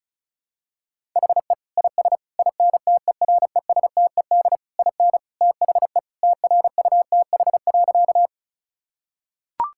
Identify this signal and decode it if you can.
Morse code, heard as a keyed tone, about 36 wpm, 700 Hz. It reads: HE IS INTERESTED IN THE TRUTH.
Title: Sentences from Top 1000 Words - Review Part 6 - 35wpm